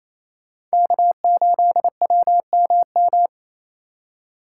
Morse code, heard as a keyed tone, about 34 wpm, 700 Hz. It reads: K8WMM
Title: Speed Racing - USA 1x3 Call Signs 28wpm